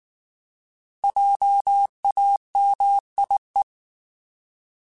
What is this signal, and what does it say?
Morse code, heard as a keyed tone, about 19 wpm, 800 Hz. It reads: JAMIE
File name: Alphabetical / Alphabetical mp3s / Js @18